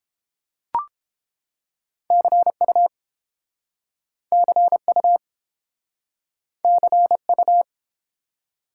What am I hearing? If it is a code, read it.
Morse code, heard as a keyed tone, about 32 wpm, 700 Hz. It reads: CU CU CU